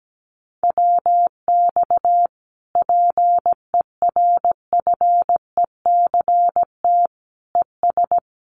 Morse code, heard as a keyed tone, about 17 wpm, 700 Hz. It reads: WX PERFECT ES
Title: Common QSO Phrases Encore 17wpm